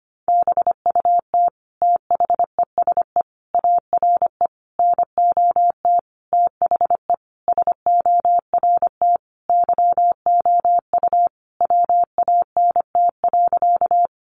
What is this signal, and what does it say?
Morse code, heard as a keyed tone, about 25 wpm, 700 Hz. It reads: BUT THESE ARE NOT THE SORT YOU WANT.
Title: Sentences from Top 1300 Words 25wpm